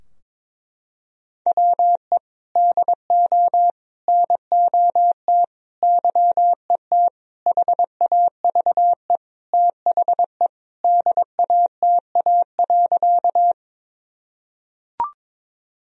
Morse code, heard as a keyed tone, about 22 wpm, 700 Hz. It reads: WE DO NOT YET HAVE THE DATA.
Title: Sentences from Top 1000 Words - Review Part 3 - 22wpm